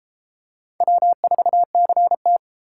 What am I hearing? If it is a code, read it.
Morse code, heard as a keyed tone, about 31 wpm, 700 Hz. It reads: W4CT